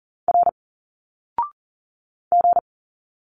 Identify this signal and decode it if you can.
Morse code, heard as a keyed tone, about 40 wpm, 700 Hz. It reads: R G
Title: Single Letters 40wpm